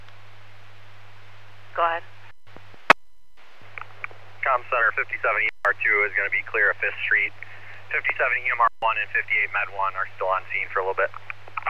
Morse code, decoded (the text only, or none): none